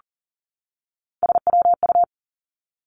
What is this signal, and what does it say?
Morse code, heard as a keyed tone, about 39 wpm, 700 Hz. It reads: SWU
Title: Three Challenging Letter-Number Combinations 40wpm